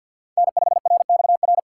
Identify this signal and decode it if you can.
Morse code, heard as a keyed tone, about 53 wpm, 700 Hz. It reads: N5RXR
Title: USA Call Signs - 6x Character Spacing - 50wpm